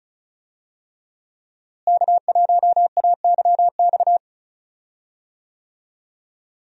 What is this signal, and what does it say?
Morse code, heard as a keyed tone, about 36 wpm, 700 Hz. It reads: K1AYX
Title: USA Call Signs - 5 Characters - Repeated 3x - 35wpm